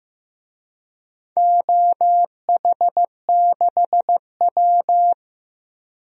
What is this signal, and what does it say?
Morse code, heard as a keyed tone, about 15 wpm, 700 Hz. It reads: OH6W
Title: DXCC-100 Call Signs - 4 Characters - Repeated 2x - 15wpm